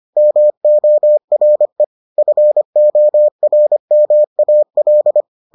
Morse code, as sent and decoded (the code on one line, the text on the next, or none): -- --- .-. / ..-. --- .-. -- .- .-..
MORE FORMAL